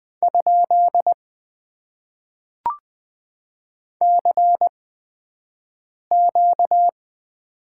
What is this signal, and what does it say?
Morse code, heard as a keyed tone, about 20 wpm, spaced as for 5 wpm, 700 Hz. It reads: ? CQ
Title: Top 90 QSO Elements - 8x Character Spacing - 20wpm